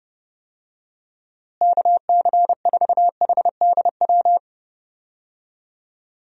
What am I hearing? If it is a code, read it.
Morse code, heard as a keyed tone, about 32 wpm, 700 Hz. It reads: KC4HDW
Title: USA Call Signs - 6 Characters - Repeated 2x - 30wpm